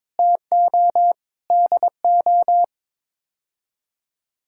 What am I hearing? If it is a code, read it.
Morse code, heard as a keyed tone, about 22 wpm, 700 Hz. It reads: TO DO
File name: practice / 22wpm / Build A Phrase - 5-Word Phrases - 22wpm